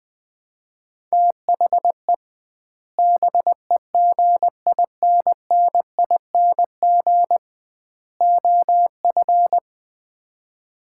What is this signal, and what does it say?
Morse code, heard as a keyed tone, about 20 wpm, 700 Hz. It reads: THE BEGINNING OF
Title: 2x Word-Spacing - Sets of 3 Words - 20wpm